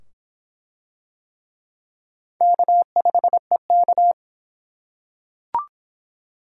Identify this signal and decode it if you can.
Morse code, heard as a keyed tone, about 25 wpm, 700 Hz. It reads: K5EK